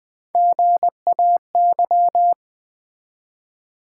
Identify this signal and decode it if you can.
Morse code, heard as a keyed tone, about 20 wpm, 700 Hz. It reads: GAY
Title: Build A Word - 1 Letter At A Time - 3-Letter Words - 20wpm